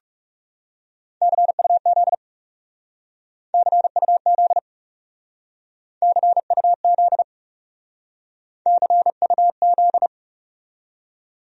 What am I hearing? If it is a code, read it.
Morse code, heard as a keyed tone, about 40 wpm, 700 Hz. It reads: CUZ CUZ CUZ CUZ